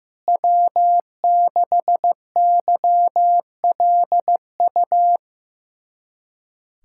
Morse code, heard as a keyed tone, about 15 wpm, 700 Hz. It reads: W6YLU
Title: USA Callsign DE Callsign 15wpm